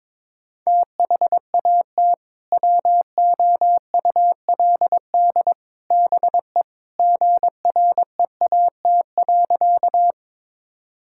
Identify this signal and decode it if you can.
Morse code, heard as a keyed tone, about 22 wpm, 700 Hz. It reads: THAT WOULD BE GREAT.
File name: practice / 22wpm / Sentences with 2-Word Phrases - 22wpm